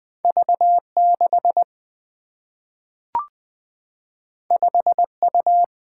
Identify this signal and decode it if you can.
Morse code, heard as a keyed tone, about 20 wpm, 700 Hz. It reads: V6 5U
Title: Single-Letter Single-Number 20wpm